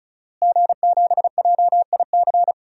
Morse code, heard as a keyed tone, about 34 wpm, 700 Hz. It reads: G7JIC
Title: DXCC-100 Call Signs - 5 Characters - Repeated 2x - 35wpm